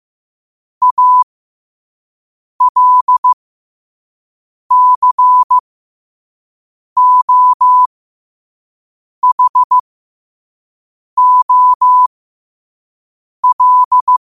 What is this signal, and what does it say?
Morse code, heard as a keyed tone, about 15 wpm, spaced as for 5 wpm, 1000 Hz. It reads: ALCOHOL
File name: qotdfiles / 5wpm